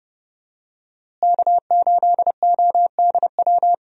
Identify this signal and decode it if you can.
Morse code, heard as a keyed tone, about 30 wpm, 700 Hz. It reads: K8ODW
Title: USA Call Signs - 5 Characters - Repeated 3x - 30wpm